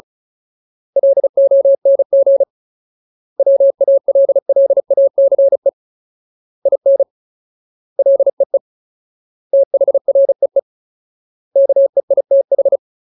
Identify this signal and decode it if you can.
Morse code, heard as a keyed tone, about 34 wpm, 550 Hz. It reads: LONG WALLACE IN LEE THREE KEITH
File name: Ultimate Word List / 35WPM Ultimate